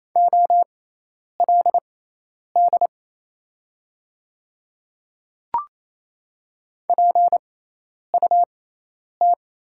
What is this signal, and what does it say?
Morse code, heard as a keyed tone, about 27 wpm, spaced as for 10 wpm, 690 Hz. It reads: OLD PUT
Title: 6x Character Spacing - Top 500 Words - 28wpm